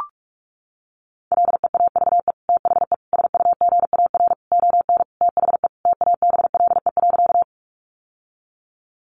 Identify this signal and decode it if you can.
Morse code, heard as a keyed tone, about 44 wpm, 700 Hz. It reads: LEAVE THE SUGAR ON THE TABLE.
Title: Sentences from Top 1200 Words 45wpm